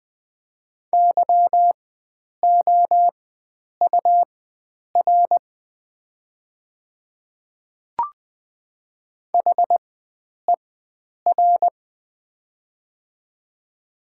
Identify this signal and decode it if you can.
Morse code, heard as a keyed tone, about 20 wpm, spaced as for 9 wpm, 700 Hz. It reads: YOUR HER
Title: 4x Character Spacing - Top 100 Words - 20wpm